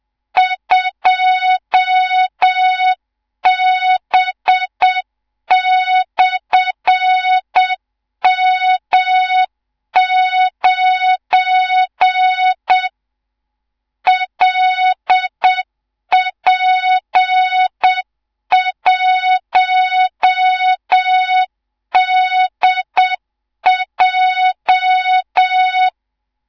Morse code, heard as a keyed tone, about 7 wpm, 3700 Hz, with noise. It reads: 2B/M9 LP1DJ
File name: Morze-HA2MN / hang / 18 IRASJELEK2